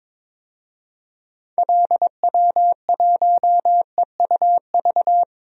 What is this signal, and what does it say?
Morse code, heard as a keyed tone, about 22 wpm, 700 Hz. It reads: LW1EUV
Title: DXCC-100 Call Signs - 6 Characters - 22wpm